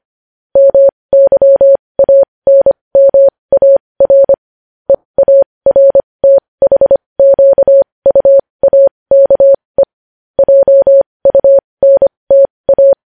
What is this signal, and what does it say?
Morse code, heard as a keyed tone, about 25 wpm, spaced as for 20 wpm, 550 Hz. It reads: MYANMAR EARTHQUAKE JUNTA
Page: News Headlines in Morse Code at 20 WPM